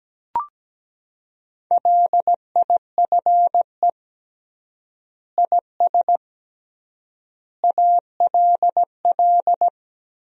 Morse code, heard as a keyed tone, about 17 wpm, 700 Hz. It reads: LIFE IS ALL
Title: 3x Word-Spacing - Sentences from Top 300 Words - 17wpm